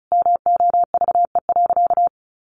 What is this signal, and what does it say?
Morse code, heard as a keyed tone, about 34 wpm, 700 Hz. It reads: MOVE.
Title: A Sound Of Thunder 35wpm